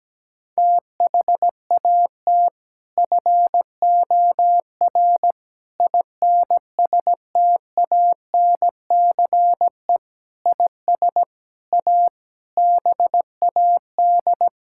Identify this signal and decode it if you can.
Morse code, heard as a keyed tone, about 17 wpm, 700 Hz. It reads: THAT FOR INSTANCE IS A BAD
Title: Sentences from Top 1000 Words - Review Part 5 - 17wpm